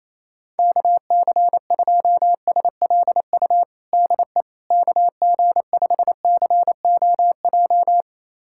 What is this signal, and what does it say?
Morse code, heard as a keyed tone, about 28 wpm, 700 Hz. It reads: KC2SLU DE KG5COJ
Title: USA Callsign DE Callsign - 6 Characters - 28wpm